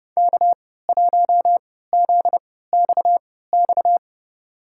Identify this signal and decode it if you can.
Morse code, heard as a keyed tone, about 32 wpm, spaced as for 17 wpm, 700 Hz. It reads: K1ZXX
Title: USA Call Signs - 5 Characters - 3x Character Spacing - 30wpm